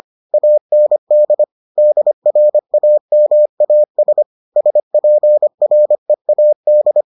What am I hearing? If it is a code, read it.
Morse code, heard as a keyed tone, about 25 wpm, 600 Hz. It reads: AND DRAMAS SPREAD